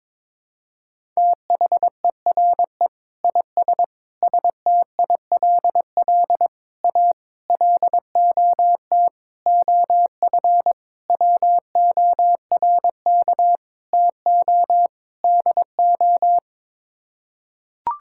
Morse code, heard as a keyed tone, about 22 wpm, 700 Hz. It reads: THERE IS STILL A LOT OF WORK TO DO
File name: Sets of 9 Words - 22wpm